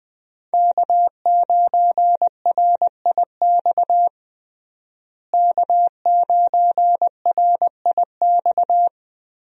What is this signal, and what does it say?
Morse code, heard as a keyed tone, about 20 wpm, 700 Hz. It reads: K9RIX K9RIX
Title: USA Call Signs - 5 Characters - Repeated 2x - 20wpm